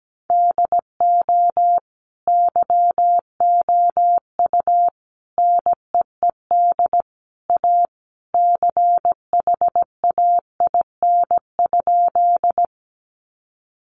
Morse code, heard as a keyed tone, about 17 wpm, 700 Hz. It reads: DO YOU NEED A CHAIN?
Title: Sentences from Top 1500 Words 17wpm